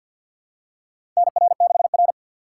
Morse code, heard as a keyed tone, about 50 wpm, 700 Hz. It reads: NR6R
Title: USA Call Signs - 4 Characters - 2x Character Spacing - 50wpm